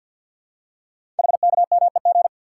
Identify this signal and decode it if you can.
Morse code, heard as a keyed tone, about 46 wpm, 700 Hz. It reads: HKMEG
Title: Code Groups - 5 Letters - 50wpm